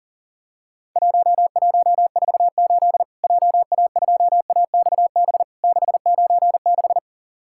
Morse code, heard as a keyed tone, about 40 wpm, 700 Hz. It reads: 1148 JA2AXB 696
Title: Worked All Europe DX Contest - QTC Exchange - 40wpm